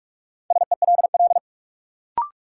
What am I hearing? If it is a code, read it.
Morse code, heard as a keyed tone, about 44 wpm, 700 Hz. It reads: SELL
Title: Sentences from Top 1100 Words - New Word Drill Alt - 45wpm